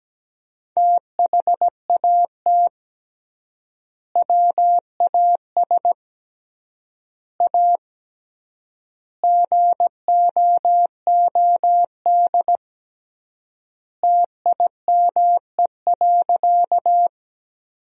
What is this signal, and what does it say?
Morse code, heard as a keyed tone, about 17 wpm, 700 Hz. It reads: THAT WAS A GOOD TIME.
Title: 3x Word-Spacing - Sentences from Top 200 Words - 17wpm